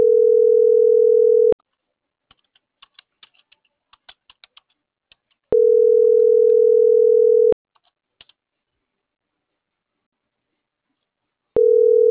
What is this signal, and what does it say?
Ringback tone (North America), 440 + 480 Hz.